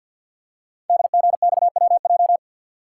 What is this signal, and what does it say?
Morse code, heard as a keyed tone, about 50 wpm, 700 Hz. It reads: DGXWJ
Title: Code Groups - 5 Letters - 50wpm